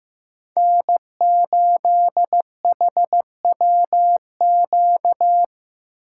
Morse code, heard as a keyed tone, about 17 wpm, 700 Hz. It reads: N8HWQ
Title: Speed Racing - USA 1x3 Call Signs 15wpm